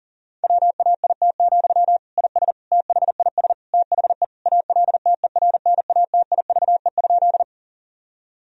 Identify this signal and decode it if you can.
Morse code, heard as a keyed tone, about 40 wpm, 700 Hz. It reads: WAIT, IS THIS THE ALTERNATIVE?
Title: Sentences from Top 1000 Words - Review Part 4 - 40wpm